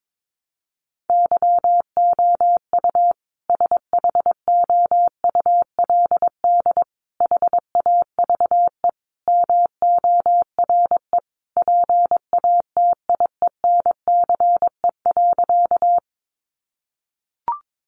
Morse code, heard as a keyed tone, about 22 wpm, 700 Hz. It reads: YOU SHOULD HAVE MORE PATIENCE.
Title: Sentences from Top 1300 Words 22wpm